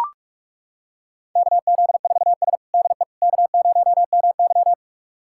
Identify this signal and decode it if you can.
Morse code, heard as a keyed tone, about 42 wpm, 700 Hz. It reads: KZ4S DE K0MY
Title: USA Callsign DE Callsign - 4 Characters - 45wpm